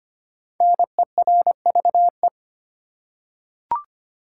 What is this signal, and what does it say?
Morse code, heard as a keyed tone, about 25 wpm, 700 Hz. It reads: NERVE
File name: 5 Letter Words - First 3 Letters Given - 25wpm